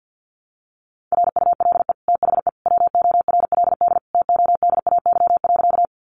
Morse code, heard as a keyed tone, about 50 wpm, 700 Hz. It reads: RULE THE WORLD TODAY.
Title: Sentences from Top 600 Words 50wpm